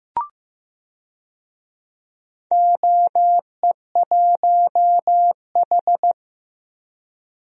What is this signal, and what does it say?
Morse code, heard as a keyed tone, about 15 wpm, 700 Hz. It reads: OE1H
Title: DXCC-100 Call Signs - 4 Characters - Repeated 3x - 15wpm